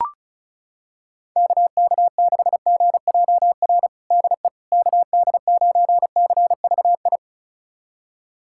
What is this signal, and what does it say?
Morse code, heard as a keyed tone, about 34 wpm, 690 Hz. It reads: KK6GJR DE KD9CVI
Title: USA Callsign DE Callsign - 6 Characters - 35wpm